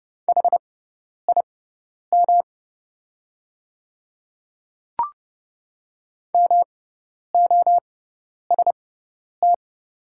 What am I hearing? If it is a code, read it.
Morse code, heard as a keyed tone, about 31 wpm, spaced as for 10 wpm, 690 Hz. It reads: HIM MOST
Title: 6x Character Spacing - Top 100 Words - 30wpm